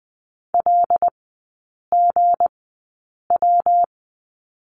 Morse code, heard as a keyed tone, about 20 wpm, spaced as for 8 wpm, 690 Hz. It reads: LGW
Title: 159 Lesson 26e - Letter g - Three Characters 20wpm